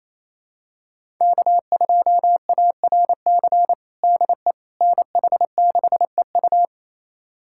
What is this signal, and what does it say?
Morse code, heard as a keyed tone, about 27 wpm, 700 Hz. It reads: K2ARC DE NH6EU
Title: USA Callsign DE Callsign - 5 Characters - 28wpm